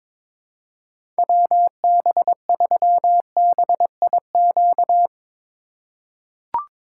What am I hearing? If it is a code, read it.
Morse code, heard as a keyed tone, about 22 wpm, 700 Hz. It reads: WB3BIQ
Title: K1USN SST QSO Practice - 22wpm